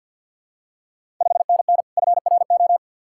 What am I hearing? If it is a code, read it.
Morse code, heard as a keyed tone, about 48 wpm, 690 Hz. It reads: 5NN FRO
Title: Ontario QSO Party Exchange 50wpm